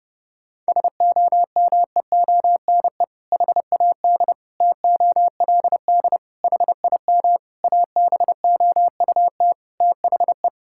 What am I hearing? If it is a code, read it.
Morse code, heard as a keyed tone, about 30 wpm, 700 Hz. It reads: SOMEONE HAD TOLD HIM ABOUT THE
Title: Sentences from Top 1000 Words - Review Part 6 - 30wpm